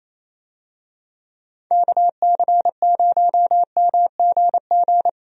Morse code, heard as a keyed tone, about 28 wpm, 700 Hz. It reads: KC0MGG
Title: USA Call Signs - 6 Characters - Repeated 2x - 28wpm